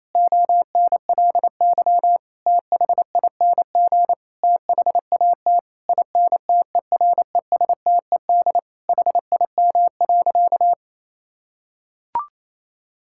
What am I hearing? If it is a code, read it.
Morse code, heard as a keyed tone, about 28 wpm, 700 Hz. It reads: ONLY THING THAT INTERESTED HIM.
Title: Sentences from Top 1000 Words - Review Part 6 - 28wpm